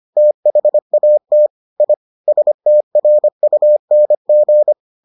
Morse code, heard as a keyed tone, about 25 wpm, 600 Hz. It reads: THAT I STRUNG